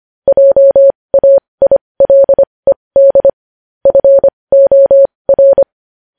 Morse code, heard as a keyed tone, about 25 wpm, spaced as for 20 wpm, 550 Hz. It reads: JAILED FOR
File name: podcast / News headlines-1x-25x20